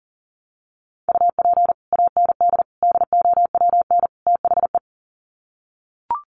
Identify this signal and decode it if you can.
Morse code, heard as a keyed tone, about 40 wpm, 700 Hz. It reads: UP AND DOWN THE
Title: Sets of 4 Words 40wpm